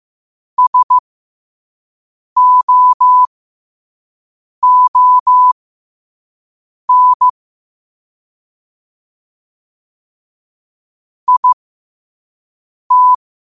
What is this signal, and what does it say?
Morse code, heard as a keyed tone, about 15 wpm, spaced as for 5 wpm, 1000 Hz. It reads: SOON IT
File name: qotdfiles / 5wpm